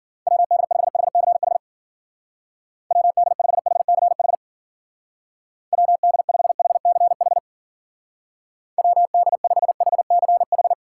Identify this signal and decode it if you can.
Morse code, heard as a keyed tone, about 49 wpm, 700 Hz. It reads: WD5HCH WD5HCH WD5HCH WD5HCH
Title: Speed Racing - USA 2x3 Call Signs 40wpm